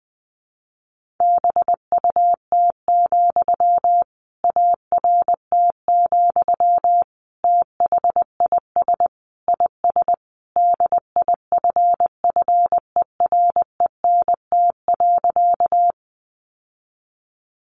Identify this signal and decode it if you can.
Morse code, heard as a keyed tone, about 20 wpm, 700 Hz. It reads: BUT, ART, THIS IS DIFFERENT.
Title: Sentences from Top 600 Words 20wpm